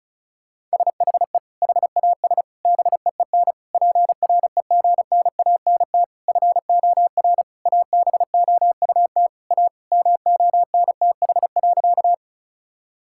Morse code, heard as a keyed tone, about 34 wpm, 700 Hz. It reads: SHE HAS BEEN PREGNANT FOR ABOUT A MONTH.
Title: Sentences from Top 1000 Words - Review Part 6 - 35wpm